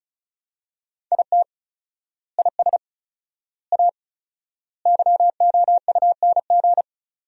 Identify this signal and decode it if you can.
Morse code, heard as a keyed tone, about 36 wpm, 700 Hz. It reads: IT IS A YOUNG
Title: 4x Word-Spacing - Sentences from Top 400 Words - 35wpm